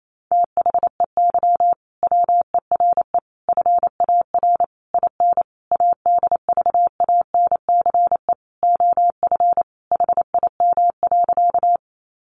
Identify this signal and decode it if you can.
Morse code, heard as a keyed tone, about 28 wpm, 700 Hz. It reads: THEY WERE FAR IN ADVANCE OF HIM.
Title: Sentences from Top 1400 Words 28wpm